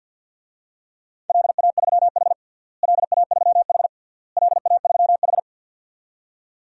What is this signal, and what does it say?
Morse code, heard as a keyed tone, about 53 wpm, 700 Hz. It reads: LA3H LA3H LA3H
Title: DXCC-100 Call Signs - 4 Characters - Repeated 3x - 50wpm